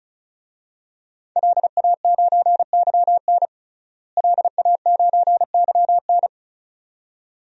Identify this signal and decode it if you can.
Morse code, heard as a keyed tone, about 34 wpm, 700 Hz. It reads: LA9YN LA9YN
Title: DXCC-100 Call Signs - 5 Characters - Repeated 2x - 35wpm